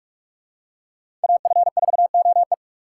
Morse code, heard as a keyed tone, about 47 wpm, 700 Hz. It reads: AF4OE
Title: USA Call Signs - 4x Character Spacing - 45wpm